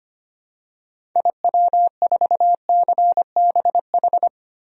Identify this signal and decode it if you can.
Morse code, heard as a keyed tone, about 25 wpm, 700 Hz. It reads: IW4CBH
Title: DXCC-100 Call Signs - 6 Characters - 25wpm